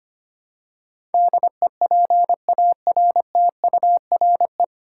Morse code, heard as a keyed tone, about 25 wpm, 700 Hz. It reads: DEPARTURE
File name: Nine Letter Words - 25wpm